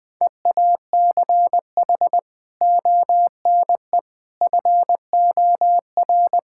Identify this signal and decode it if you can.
Morse code, heard as a keyed tone, about 20 wpm, 700 Hz. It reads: EACH ONE FOR
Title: Sentences from Top 1600 Words 20wpm